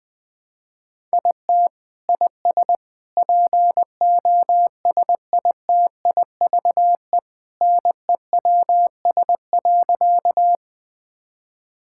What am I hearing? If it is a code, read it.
Morse code, heard as a keyed tone, about 21 wpm, 700 Hz. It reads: IT IS POSITIVE NEWS.
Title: Sentences from Top 1000 Words - Review Part 5 - 20wpm